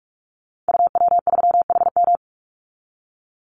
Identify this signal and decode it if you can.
Morse code, heard as a keyed tone, about 44 wpm, 700 Hz. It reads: UW3HM
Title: Call Signs Repeated 3x - 45wpm